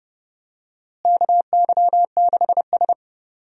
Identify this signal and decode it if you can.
Morse code, heard as a keyed tone, about 30 wpm, 700 Hz. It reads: KY6S
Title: USA Call Signs - 4 Characters - Repeated 2x - 30wpm